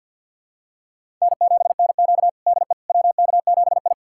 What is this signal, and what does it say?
Morse code, heard as a keyed tone, about 47 wpm, 700 Hz. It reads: N7NQ DE WK7I
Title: USA Callsign DE Callsign - 4 Characters - 50wpm